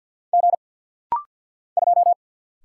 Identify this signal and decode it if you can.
Morse code, heard as a keyed tone, about 50 wpm, 700 Hz. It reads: G 2
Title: Single Letter-Number Alt - Rapid-Fire 50wpm